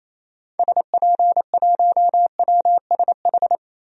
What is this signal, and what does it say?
Morse code, heard as a keyed tone, about 28 wpm, 700 Hz. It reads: SP1WSH
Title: DXCC-100 Call Signs - 6 Characters - Repeated 3x - 28wpm